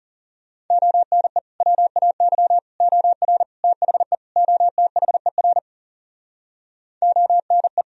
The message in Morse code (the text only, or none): ONE WAY OR THE OTHER ONE